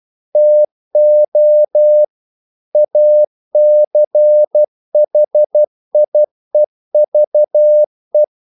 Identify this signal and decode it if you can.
Morse code, heard as a keyed tone, about 12 wpm, 600 Hz. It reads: TO ACHIEVE
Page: English quotations in CW (telegraphy - morse)